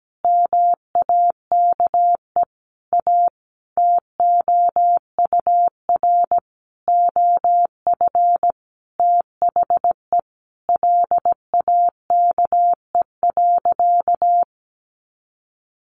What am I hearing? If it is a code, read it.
Morse code, heard as a keyed tone, about 17 wpm, 700 Hz. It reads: MAKE A TOUR OF THE LAKE.
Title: Sentences from Top 1400 Words 17wpm